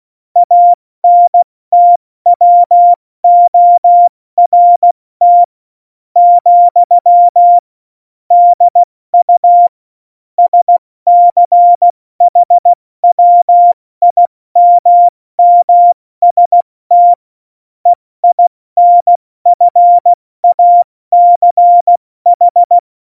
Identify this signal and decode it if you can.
Morse code, heard as a keyed tone, about 16 wpm, 700 Hz. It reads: ANTWORT , DU SCHWIMMST EINFACH
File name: Morsekurs / Witze / Witz 30. Oktober 2018, 70 F80